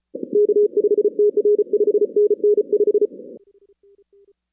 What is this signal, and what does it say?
Morse code, heard as a keyed tone, about 34 wpm, 420 Hz. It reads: K5TR 5NN 5
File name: 2016 arrl dx cw / 2016-ARRL-DX-CW-K5TR- - 0-17